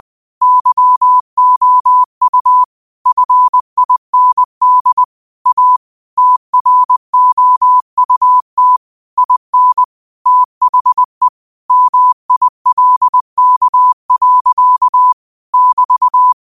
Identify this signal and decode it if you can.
Morse code, heard as a keyed tone, about 20 wpm, 1000 Hz. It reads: YOU FIND A TROUT IN THE MILK. <BT>